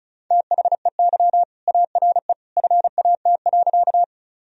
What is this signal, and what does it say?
Morse code, heard as a keyed tone, about 36 wpm, 700 Hz. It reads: THEY ARE FAT.
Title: Sentences from Top 1000 Words - Review Part 4 - 35wpm